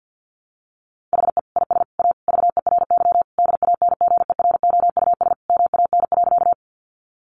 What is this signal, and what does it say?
Morse code, heard as a keyed tone, about 50 wpm, 700 Hz. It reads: HE IS A VERY DANGEROUS MAN.
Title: Sentences from Top 1200 Words 50wpm